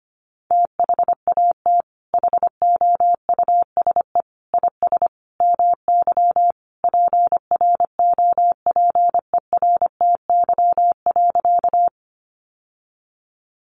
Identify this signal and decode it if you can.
Morse code, heard as a keyed tone, about 25 wpm, 700 Hz. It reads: THAT HOUSE IS MY PROPERTY.
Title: Sentences from Top 1000 Words 25wpm